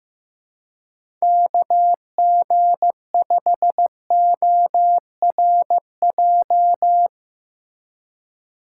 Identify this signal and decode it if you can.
Morse code, heard as a keyed tone, about 15 wpm, 700 Hz. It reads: KG5ORJ
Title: USA Call Signs - 6 Characters - 6x Character Spacing - 15wpm